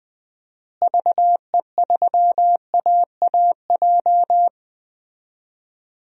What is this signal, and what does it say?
Morse code, heard as a keyed tone, about 20 wpm, 700 Hz. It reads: VE3AAJ
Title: DXCC-100 Call Signs - 6 Characters - Repeated 3x - 20wpm